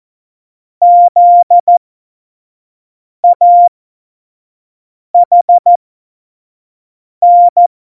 Morse code, heard as a keyed tone, about 14 wpm, spaced as for 5 wpm, 700 Hz. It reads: ZAHN
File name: Morsekurs / Witze / Witz 19. Dezember 2017, 25 F70